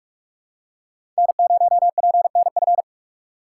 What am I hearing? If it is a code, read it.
Morse code, heard as a keyed tone, about 44 wpm, 700 Hz. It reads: N0PNF
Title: USA Call Signs - 2x Character Spacing - 45wpm